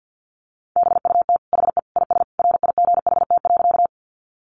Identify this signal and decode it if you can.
Morse code, heard as a keyed tone, about 50 wpm, 700 Hz. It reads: BUT HE IS RIGHT.
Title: Sentences from Top 200 Words 50wpm